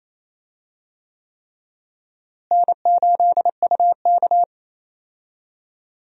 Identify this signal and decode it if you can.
Morse code, heard as a keyed tone, about 28 wpm, 700 Hz. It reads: N8UK
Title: USA Call Signs - 4 Characters - Repeated 3x - 28wpm